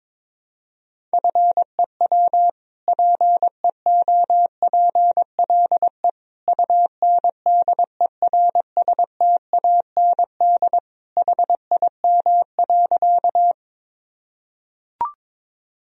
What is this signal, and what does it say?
Morse code, heard as a keyed tone, about 22 wpm, 700 Hz. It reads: FEW PEOPLE UNDERSTAND HIM.
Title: Sentences from Top 1000 Words - Review Part 1 - 22wpm